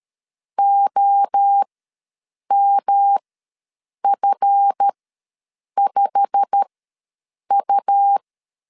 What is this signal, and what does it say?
Morse code, heard as a keyed tone, about 13 wpm, spaced as for 7 wpm, 800 Hz. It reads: OMF5U
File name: LEZIONI / 8 YJ45 / 8 YJ45 ws 12wpm 3146